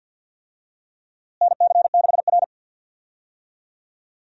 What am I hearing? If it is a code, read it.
Morse code, heard as a keyed tone, about 51 wpm, 700 Hz. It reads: NC6R
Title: USA Call Signs - 4 Characters - Repeated 3x - 50wpm